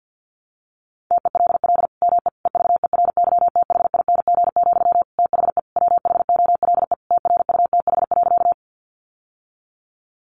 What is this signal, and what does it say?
Morse code, heard as a keyed tone, about 50 wpm, 700 Hz. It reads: TELL ME EVERYTHING, THE WHOLE TRUTH.
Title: Sentences from Top 800 Words 50wpm